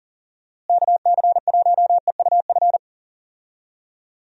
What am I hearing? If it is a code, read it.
Morse code, heard as a keyed tone, about 40 wpm, 700 Hz. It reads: KC1EUF
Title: USA Call Signs - 6 Characters - Repeated 2x - 40wpm